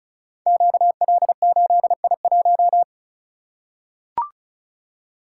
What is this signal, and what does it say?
Morse code, heard as a keyed tone, about 36 wpm, 700 Hz. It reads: QL8I1
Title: Code Groups - 5 Letter:Numbers - 35wpm